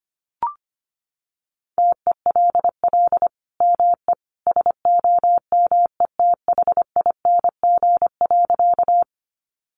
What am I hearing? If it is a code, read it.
Morse code, heard as a keyed tone, about 25 wpm, 700 Hz. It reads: TELL ME SOMETHING.
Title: Sentences from Top 600 Words 25wpm